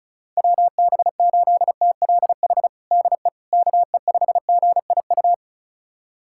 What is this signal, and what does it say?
Morse code, heard as a keyed tone, about 36 wpm, 700 Hz. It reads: WB8TLH DE KE5GIU
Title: USA Callsign DE Callsign - 6 Characters - 35wpm